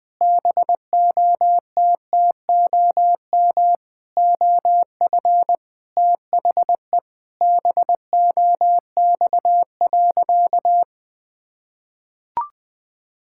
Morse code, heard as a keyed tone, about 20 wpm, 700 Hz. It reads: BOTTOM OF THE BOX.
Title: Sentences with 5-Word Phrases - 20wpm